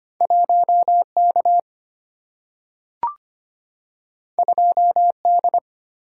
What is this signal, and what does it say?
Morse code, heard as a keyed tone, about 25 wpm, 700 Hz. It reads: 1K 2D
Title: Single-Letter Single-Number 25wpm